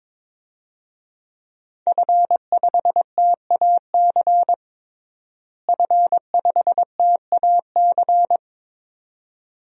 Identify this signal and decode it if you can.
Morse code, heard as a keyed tone, about 22 wpm, 700 Hz. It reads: F5TAC F5TAC
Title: DXCC-100 Call Signs - 5 Characters - Repeated 2x - 22wpm